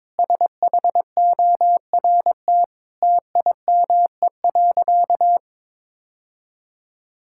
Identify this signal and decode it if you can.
Morse code, heard as a keyed tone, about 22 wpm, 700 Hz. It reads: SHORT TIME.
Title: Sentences from Top 1600 Words 22wpm